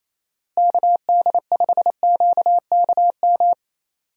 Morse code, heard as a keyed tone, about 28 wpm, 700 Hz. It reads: KD5QKM
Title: USA Call Signs - 6 Characters - Repeated 3x - 28wpm